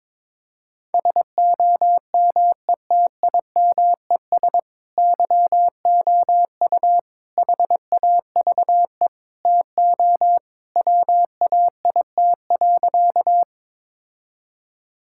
Morse code, heard as a keyed tone, about 22 wpm, 700 Hz. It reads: SOMETIMES YOU HAVE TO WAIT.
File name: practice / 22wpm / Sentences with 3-Word Phrases - 22wpm